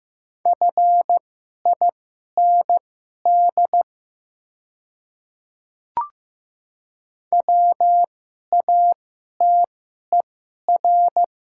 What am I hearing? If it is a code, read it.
Morse code, heard as a keyed tone, about 15 wpm, spaced as for 11 wpm, 700 Hz. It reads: FIND WATER
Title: 2x Character Spacing - Top 100 Words - 15wpm